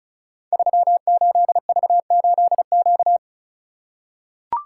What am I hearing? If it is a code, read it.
Morse code, heard as a keyed tone, about 36 wpm, 700 Hz. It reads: 38V8Q
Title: Code Groups - 5 Letter:Numbers - 35wpm